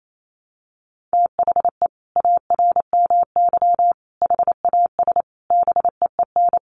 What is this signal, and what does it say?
Morse code, heard as a keyed tone, about 28 wpm, 700 Hz. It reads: THE ARMY HAS BEEN
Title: Sentences from Top 1400 Words 28wpm